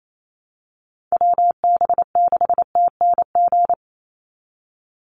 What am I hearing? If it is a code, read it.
Morse code, heard as a keyed tone, about 28 wpm, 700 Hz. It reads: WB6TNG
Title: USA Call Signs 28wpm